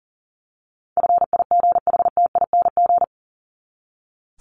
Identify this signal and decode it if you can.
Morse code, heard as a keyed tone, about 40 wpm, 700 Hz. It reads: FIGHTING
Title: ING Suffix - Top 100 Words with Base Words 40wpm